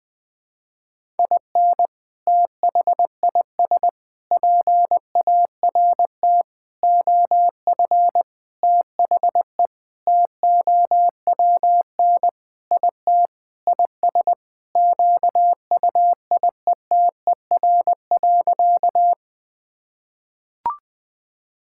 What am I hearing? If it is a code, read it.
Morse code, heard as a keyed tone, about 20 wpm, 700 Hz. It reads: IN THIS PART OF THE TOWN IT IS QUIETER.
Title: Sentences with 5-Word Phrases - 20wpm